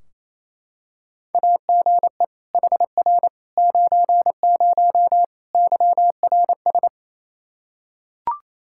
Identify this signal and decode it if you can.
Morse code, heard as a keyed tone, about 28 wpm, 700 Hz. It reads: AGE HR 90 YRS
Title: Practice QSO - Exchange 3 - More Info - 28wpm